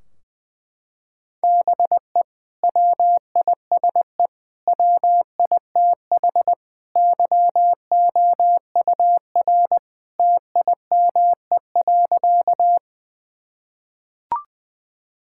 Morse code, heard as a keyed tone, about 21 wpm, 700 Hz. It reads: BE WISE WITH YOUR TIME.
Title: Sentences from Top 1600 Words 20wpm